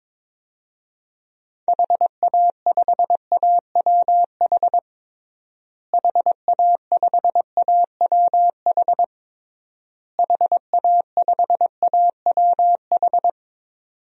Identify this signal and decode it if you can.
Morse code, heard as a keyed tone, about 22 wpm, 700 Hz. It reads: HA5AWH HA5AWH HA5AWH
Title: DXCC-100 Call Signs - 6 Characters - Repeated 3x - 22wpm